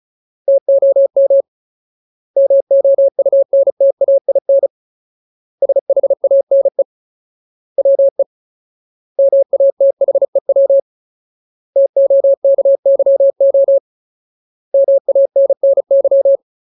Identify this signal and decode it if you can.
Morse code, heard as a keyed tone, about 35 wpm, 550 Hz. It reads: TOM MOUNTAIN SHANE WE MATHEW TOKYO MANNY
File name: Ultimate Word List / 35WPM Ultimate